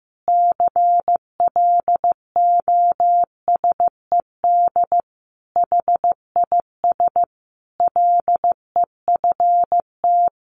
Morse code, heard as a keyed tone, about 15 wpm, 700 Hz. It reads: CLOSED HIS LEFT
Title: Sentences from Top 1400 Words 15wpm